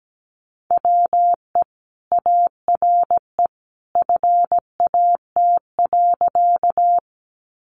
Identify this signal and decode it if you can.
Morse code, heard as a keyed tone, about 17 wpm, 700 Hz. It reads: WE ARE FAT.
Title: Sentences from Top 600 Words 17wpm